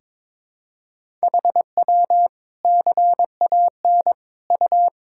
Morse code, heard as a keyed tone, about 22 wpm, 700 Hz. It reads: HW CAN U
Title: Sentences from Top 200 Words and Abbreviations 22wpm